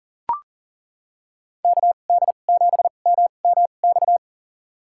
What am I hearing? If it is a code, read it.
Morse code, heard as a keyed tone, about 37 wpm, spaced as for 28 wpm, 700 Hz. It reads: KD7MMX
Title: USA Call Signs - 6 Characters - 2x Character Spacing - 40wpm